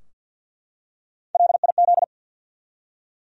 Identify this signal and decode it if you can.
Morse code, heard as a keyed tone, about 53 wpm, 700 Hz. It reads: LIZ
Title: USA Common Names - Max 4 Letters - 50wpm